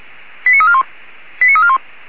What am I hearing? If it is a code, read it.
Five-tone selective call (ZVEI), heard as quick runs of five tones, tones of about 70 ms. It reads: 89341 89341
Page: /464/SDR/ZVEI/89335/